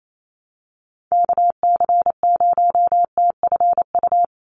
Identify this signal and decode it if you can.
Morse code, heard as a keyed tone, about 28 wpm, 700 Hz. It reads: KC0TFU
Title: USA Call Signs 28wpm